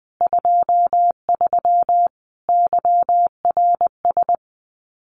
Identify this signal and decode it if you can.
Morse code, heard as a keyed tone, about 20 wpm, 700 Hz. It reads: 23 YRS
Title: Common QSO Phrases Encore 20wpm